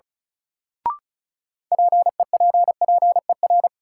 Morse code, heard as a keyed tone, about 36 wpm, 700 Hz. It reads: PEPPER
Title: Sets of 5 Words - New Word Drill - 35wpm